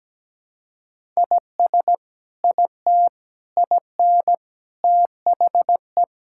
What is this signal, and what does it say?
Morse code, heard as a keyed tone, about 17 wpm, 700 Hz. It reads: IS IT IN THE
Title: Sentences from Top 1000 Words - Review Part 5 - 17wpm